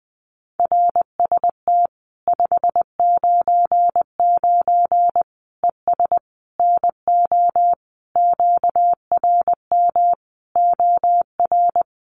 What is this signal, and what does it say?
Morse code, heard as a keyed tone, about 20 wpm, 700 Hz. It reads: RST 599 ES NO QRM OR
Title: Common QSO Phrases 20wpm